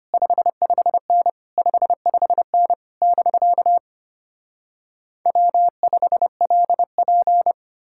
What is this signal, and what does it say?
Morse code, heard as a keyed tone, about 33 wpm, 700 Hz. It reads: 55N 55N <BK> W5LP